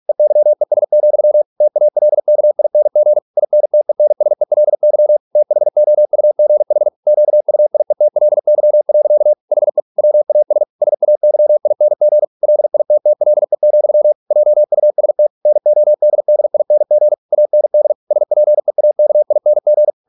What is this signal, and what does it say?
Morse code, heard as a keyed tone, about 46 wpm, 600 Hz. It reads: EYES, TALKING INTENSELY THOUGH QUIETLY. HE WAS SAYING LITTLE, JUST NODDING AND SPEAKING